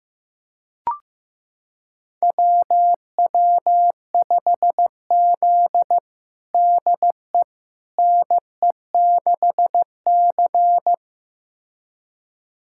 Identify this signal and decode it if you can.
Morse code, heard as a keyed tone, about 16 wpm, 700 Hz. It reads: WW5Z DE NE6C
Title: USA Callsign DE Callsign - 4 Characters - 15wpm